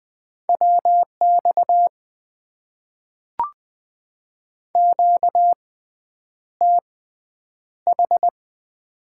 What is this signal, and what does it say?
Morse code, heard as a keyed tone, about 20 wpm, spaced as for 9 wpm, 700 Hz. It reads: WX QTH
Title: Top 90 QSO Elements - 6x Character Spacing - 20wpm